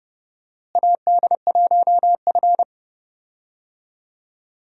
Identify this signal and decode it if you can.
Morse code, heard as a keyed tone, about 32 wpm, 700 Hz. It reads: AD1F
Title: USA Call Signs - 4 Characters - Repeated 2x - 30wpm